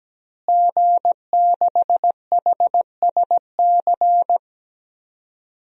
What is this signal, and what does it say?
Morse code, heard as a keyed tone, about 17 wpm, 700 Hz. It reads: G6HSC
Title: DXCC-100 Call Signs - 5 Characters - Repeated 2x - 17wpm